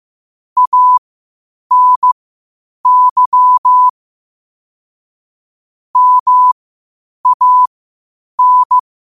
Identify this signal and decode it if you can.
Morse code, heard as a keyed tone, about 15 wpm, spaced as for 8 wpm, 1000 Hz. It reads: ANY MAN
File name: qotdfiles / 8wpm